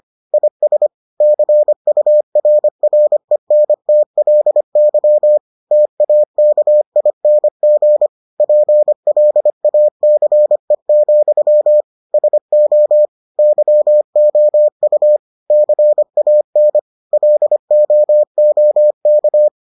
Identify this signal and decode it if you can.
Morse code, heard as a keyed tone, about 25 wpm, 600 Hz. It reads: IS CURRENTLY TAKING PLACE, SO YOU CAN LOOK